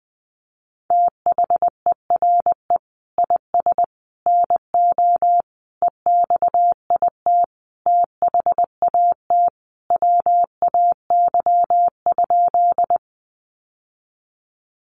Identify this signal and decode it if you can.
Morse code, heard as a keyed tone, about 20 wpm, 700 Hz. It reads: THERE IS NO EXIT THAT WAY?
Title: Sentences from Top 1100 Words 20wpm